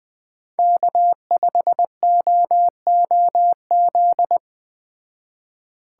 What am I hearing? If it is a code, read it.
Morse code, heard as a keyed tone, about 21 wpm, 700 Hz. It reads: K5OOZ
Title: Speed Racing - USA 1x3 Call Signs 17wpm